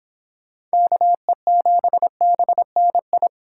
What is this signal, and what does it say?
Morse code, heard as a keyed tone, about 25 wpm, 700 Hz. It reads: KE7BNI